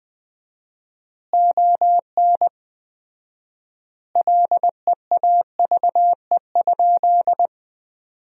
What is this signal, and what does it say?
Morse code, heard as a keyed tone, about 20 wpm, 700 Hz. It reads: ON LEAVE?
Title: 4x Word-Spacing - Sentences from Top 400 Words - 20wpm